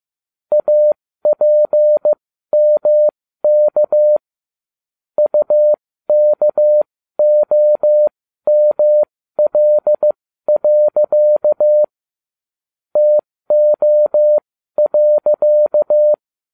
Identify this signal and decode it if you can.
Morse code, heard as a keyed tone, about 15 wpm, spaced as for 13 wpm, 600 Hz. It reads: APMK UKOML. TO.